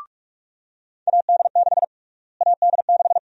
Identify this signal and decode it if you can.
Morse code, heard as a keyed tone, about 47 wpm, 700 Hz. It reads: AD6 AD6
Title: USA Call Sign Prefixes - Repeated 2x - 45wpm